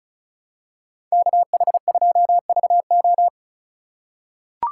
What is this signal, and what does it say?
Morse code, heard as a keyed tone, about 34 wpm, 700 Hz. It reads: KH2VO
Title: USA Call Signs - 5 Characters - 2x Character Spacing - 35wpm